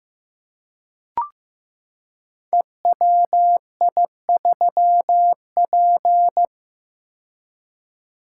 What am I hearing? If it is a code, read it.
Morse code, heard as a keyed tone, about 15 wpm, 700 Hz. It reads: EWI3P
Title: Code Groups - 5 Letter:Numbers - 15wpm